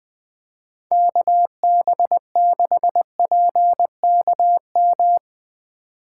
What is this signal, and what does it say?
Morse code, heard as a keyed tone, about 20 wpm, 700 Hz. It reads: KB6PKM
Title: USA Call Signs - 2x Character Spacing - 20wpm